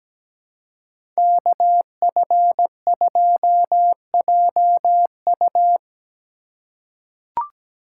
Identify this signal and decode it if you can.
Morse code, heard as a keyed tone, about 17 wpm, 700 Hz. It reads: KF2JU
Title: USA Call Signs - 3x Character Spacing - 17wpm